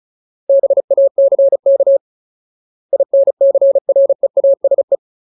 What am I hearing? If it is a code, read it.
Morse code, heard as a keyed tone, about 34 wpm, 550 Hz. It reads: BACK INCREASE
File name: Ultimate Word List / 35WPM Ultimate